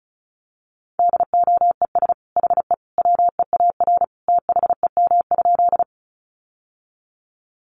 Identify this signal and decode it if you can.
Morse code, heard as a keyed tone, about 34 wpm, 700 Hz. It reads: DOES HE WEAR THEM?
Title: Sentences from Top 1500 Words 35wpm